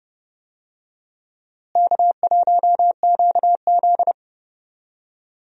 Morse code, heard as a keyed tone, about 32 wpm, 700 Hz. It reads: K1QZ
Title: USA Call Signs - 4 Characters - Repeated 2x - 30wpm